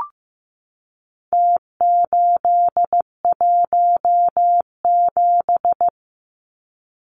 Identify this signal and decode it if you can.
Morse code, heard as a keyed tone, about 15 wpm, 700 Hz. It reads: T817
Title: Numbers - Contest Style - 4-Digit 15wpm